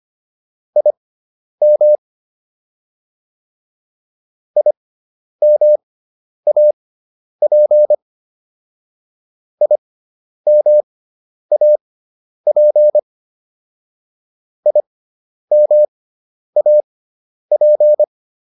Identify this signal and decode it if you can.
Morse code, heard as a keyed tone, about 25 wpm, spaced as for 10 wpm, 600 Hz. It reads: IM IMAP IMAP IMAP